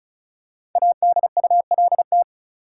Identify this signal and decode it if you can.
Morse code, heard as a keyed tone, about 36 wpm, 700 Hz. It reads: ADULT
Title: Five Letter Words - 35wpm